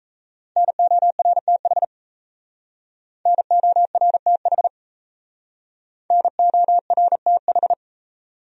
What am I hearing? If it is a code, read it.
Morse code, heard as a keyed tone, about 37 wpm, 700 Hz. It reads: NORTH NORTH NORTH